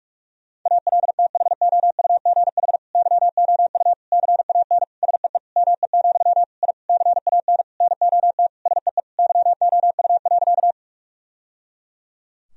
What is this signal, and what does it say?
Morse code, heard as a keyed tone, about 46 wpm, 700 Hz. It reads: ALTHOUGH YOU CAN SEE ME, I CAN NOT SEE YOU.
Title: Sentences from Top 1000 Words - Review Part 3 - 45wpm